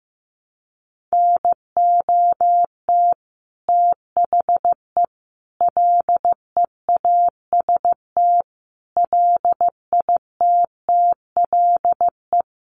Morse code, heard as a keyed tone, about 15 wpm, 700 Hz. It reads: NOT THE LEAST LITTLE
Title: Sentences from Top 900 Words 15wpm